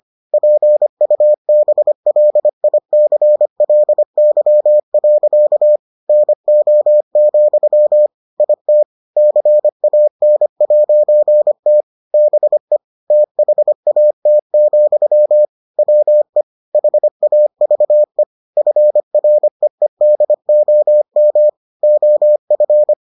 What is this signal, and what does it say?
Morse code, heard as a keyed tone, about 25 wpm, 600 Hz. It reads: PUBLICLY. NO, IT CAN'T BE THAT, WE HAVE FREEDOM OF